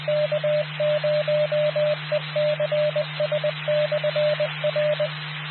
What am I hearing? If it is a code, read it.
Morse code, heard as a keyed tone, about 20 wpm, 600 Hz, with noise. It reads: K0ECS/R
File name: K0ECS 1454700MHz Johnson County RACES and Skywarn Repeater